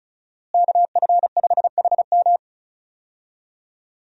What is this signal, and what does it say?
Morse code, heard as a keyed tone, about 34 wpm, 700 Hz. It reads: KF5HM
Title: USA Call Signs - 5 Characters - Repeated 3x - 35wpm